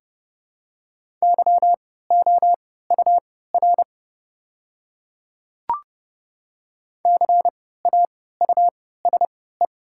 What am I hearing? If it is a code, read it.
Morse code, heard as a keyed tone, about 30 wpm, spaced as for 17 wpm, 700 Hz. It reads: YOUR CAUSE
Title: 3x Character Spacing - Top 200 Words - 30wpm